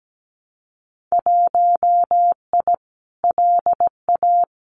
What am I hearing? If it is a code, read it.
Morse code, heard as a keyed tone, about 17 wpm, 700 Hz. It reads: 1I LA
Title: Winter Field Day Exchanges Encore 17wpm